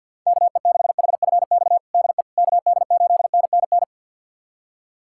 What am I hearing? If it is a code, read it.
Morse code, heard as a keyed tone, about 51 wpm, 700 Hz. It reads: KE6HFX DE KD8NNN
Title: USA Callsign DE Callsign - 6 Characters - 50wpm